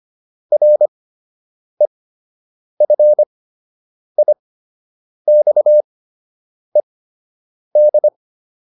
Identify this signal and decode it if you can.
Morse code, heard as a keyed tone, about 25 wpm, spaced as for 8 wpm, 600 Hz. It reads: REFIXED